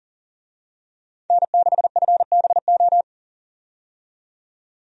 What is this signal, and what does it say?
Morse code, heard as a keyed tone, about 40 wpm, 700 Hz. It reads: N6FBO
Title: USA Call Signs - 5 Characters - 40wpm